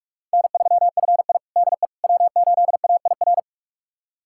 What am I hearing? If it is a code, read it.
Morse code, heard as a keyed tone, about 46 wpm, 700 Hz. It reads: N3FI DE W8AIR
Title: USA Callsign DE Callsign 45wpm